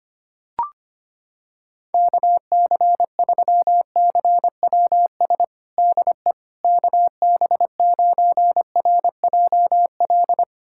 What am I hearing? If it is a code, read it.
Morse code, heard as a keyed tone, about 25 wpm, 700 Hz. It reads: KC3CWS DE KB9RJL
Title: USA Callsign DE Callsign - 6 Characters - 25wpm